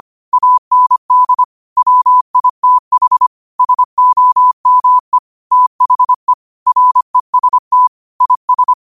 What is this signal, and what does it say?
Morse code, heard as a keyed tone, about 25 wpm, 1000 Hz. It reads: AND WITH SOME THE REST IS
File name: qotdfiles / 25wpm